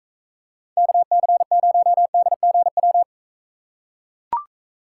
Morse code, heard as a keyed tone, about 43 wpm, 700 Hz. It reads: KC0DGW